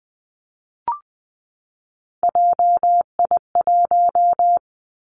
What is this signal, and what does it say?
Morse code, heard as a keyed tone, about 20 wpm, 700 Hz. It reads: JI1
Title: Three Challenging Letter-Number Combinations 20wpm